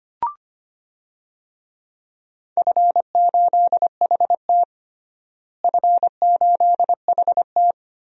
Morse code, heard as a keyed tone, about 25 wpm, 700 Hz. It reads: F8HT F8HT
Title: DXCC-100 Call Signs - 4 Characters - Repeated 3x - 25wpm